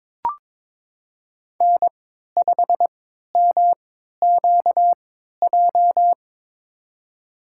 Morse code, heard as a keyed tone, about 22 wpm, spaced as for 12 wpm, 700 Hz. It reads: N5MQJ
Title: USA Call Signs - 5 Characters - 3x Character Spacing - 22wpm